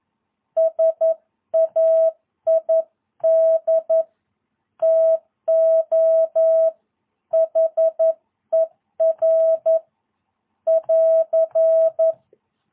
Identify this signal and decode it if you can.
Morse code, heard as a keyed tone, about 11 wpm, 660 Hz. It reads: SAID TO HER <AR>
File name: jh8cbh / 2022cw